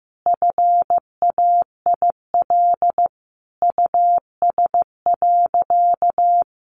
Morse code, heard as a keyed tone, about 15 wpm, 700 Hz. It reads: FAIL US.
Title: Sentences from Top 1100 Words 15wpm